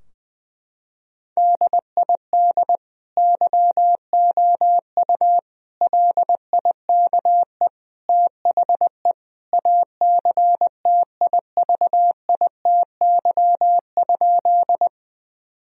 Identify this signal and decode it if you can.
Morse code, heard as a keyed tone, about 20 wpm, 700 Hz. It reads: DID YOU LIKE THE ACTIVITY?
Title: Sentences from Top 1000 Words - Review Part 4 - 20wpm